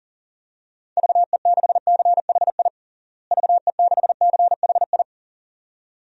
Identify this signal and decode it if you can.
Morse code, heard as a keyed tone, about 40 wpm, 700 Hz. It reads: VE6CHI VE6CHI
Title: DXCC-100 Call Signs - 6 Characters - Repeated 2x - 40wpm